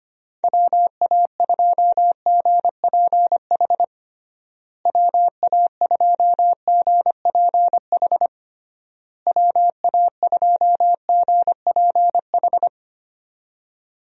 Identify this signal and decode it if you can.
Morse code, heard as a keyed tone, about 25 wpm, 700 Hz. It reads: WA2GPH WA2GPH WA2GPH
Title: USA Call Signs - 6 Characters - Repeated 3x - 25wpm